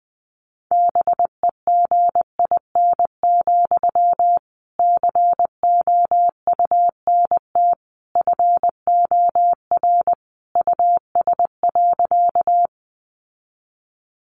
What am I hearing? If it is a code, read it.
Morse code, heard as a keyed tone, about 20 wpm, 700 Hz. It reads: BEGIN, COUNT FOR US.
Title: Sentences from Top 1500 Words 20wpm